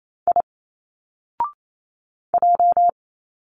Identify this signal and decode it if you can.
Morse code, heard as a keyed tone, about 28 wpm, 700 Hz. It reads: I J
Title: Single Letters 28wpm